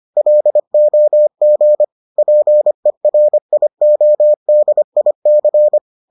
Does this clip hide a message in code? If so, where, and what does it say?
Yes: Morse code from 0.2 to 5.8 s, heard as a keyed tone, about 25 wpm, 600 Hz. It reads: LOG PERIODIC